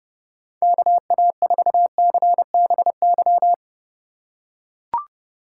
Morse code, heard as a keyed tone, about 32 wpm, 700 Hz. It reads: KA4CBY
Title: USA Call Signs - 6 Characters - 3x Character Spacing - 30wpm